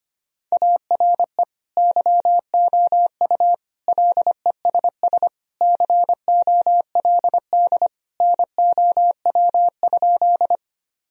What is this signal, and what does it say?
Morse code, heard as a keyed tone, about 25 wpm, 700 Hz. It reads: ARE YOU LESS COLD NOW?
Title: Sentences from Top 1000 Words - Review Part 2 - 25wpm